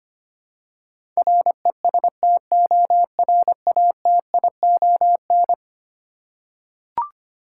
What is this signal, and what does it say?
Morse code, heard as a keyed tone, about 25 wpm, 700 Hz. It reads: RESTORATION
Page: morse-code-ninja-practice-25wpm